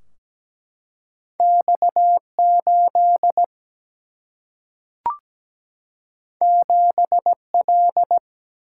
Morse code, heard as a keyed tone, about 17 wpm, 700 Hz. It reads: X8 7L
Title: Single-Letter Single-Number 17wpm